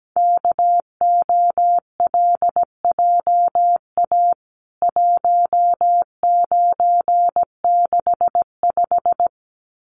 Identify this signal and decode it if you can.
Morse code, heard as a keyed tone, about 17 wpm, 700 Hz. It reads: KOLJA 1965
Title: CWT Exchange 17wpm